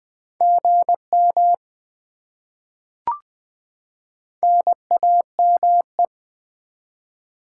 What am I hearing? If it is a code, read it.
Morse code, heard as a keyed tone, about 20 wpm, 700 Hz. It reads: GM NAME
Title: Top 20 QSO Elements - 20wpm